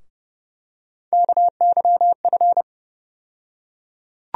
Morse code, heard as a keyed tone, about 30 wpm, 700 Hz. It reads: KYF
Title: USA Call Sign Postfixes - 3 Letters 30wpm